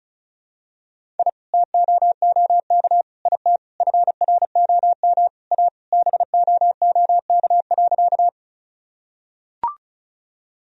Morse code, heard as a keyed tone, about 36 wpm, 700 Hz. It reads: I TOOK IT FROM A BOOK.
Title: Sentences from Top 1000 Words - Review Part 1 - 35wpm